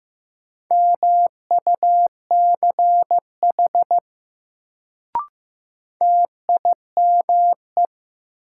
Morse code, heard as a keyed tone, about 15 wpm, 700 Hz. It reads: MUCH TIME
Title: Sets of 2 Words - New Word Drill - 15wpm